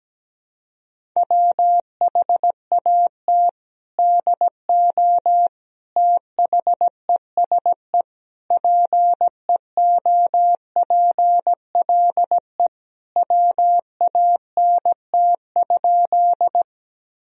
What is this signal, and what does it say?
Morse code, heard as a keyed tone, about 17 wpm, 700 Hz. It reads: WHAT DO THESE PEOPLE WANT?
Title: Sentences from Top 1000 Words - Review Part 1 - 17wpm